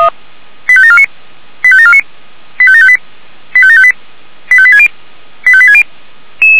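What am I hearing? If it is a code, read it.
Five-tone selective call (ZVEI), heard as quick runs of five tones, tones of about 70 ms. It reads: 85749 85749 85758 85758 85860 85860
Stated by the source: ZVEI five-tone selective call 85724 received on an SDR.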